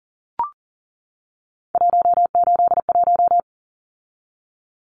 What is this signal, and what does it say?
Morse code, heard as a keyed tone, about 40 wpm, 700 Hz. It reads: 181
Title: Numbers - 3-Digit 40wpm